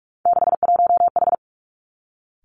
Morse code, heard as a keyed tone, about 46 wpm, 700 Hz. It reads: 61H
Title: Three Challenging Letter-Number Combinations 45wpm